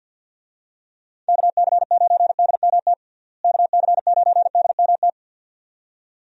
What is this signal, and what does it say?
Morse code, heard as a keyed tone, about 51 wpm, 700 Hz. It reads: KC9DMT KC9DMT
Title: USA Call Signs - 6 Characters - Repeated 2x - 50wpm